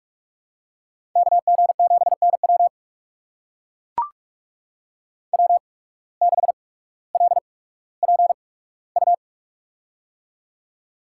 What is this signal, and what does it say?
Morse code, heard as a keyed tone, about 42 wpm, spaced as for 19 wpm, 700 Hz. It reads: KG7NW W6LPU
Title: USA Call Signs - 5 Characters - 8x Character Spacing - 45wpm